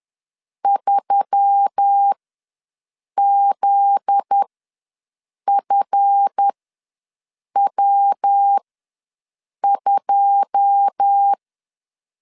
Morse code, heard as a keyed tone, about 10 wpm, spaced as for 6 wpm, 800 Hz. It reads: 3ZFW2
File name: LEZIONI / 9 2 ws 10wpm 4130